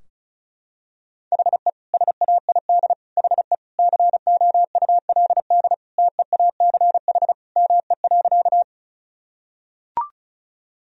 Morse code, heard as a keyed tone, about 34 wpm, 700 Hz. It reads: HE SAID HE COULD TEACH ME.
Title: Sentences from Top 1000 Words - Review Part 2 - 35wpm